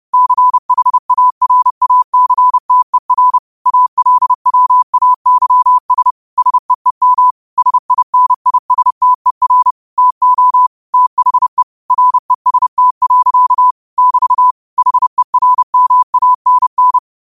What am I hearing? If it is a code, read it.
Morse code, heard as a keyed tone, about 30 wpm, 1000 Hz. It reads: CHARACTER ALWAYS SEEM SINISTER TO THE REST. <BT> HERMANN